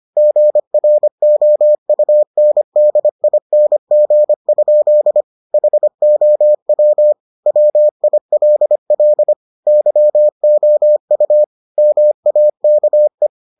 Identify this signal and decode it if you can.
Morse code, heard as a keyed tone, about 25 wpm, 600 Hz. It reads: GROUNDING? HOW WILL YOU MAKE